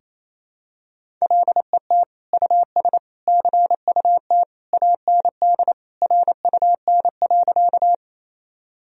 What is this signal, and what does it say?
Morse code, heard as a keyed tone, about 28 wpm, 700 Hz. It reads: LET US CUT AND RUN.
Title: Sentences from Top 1000 Words - Review Part 1 - 28wpm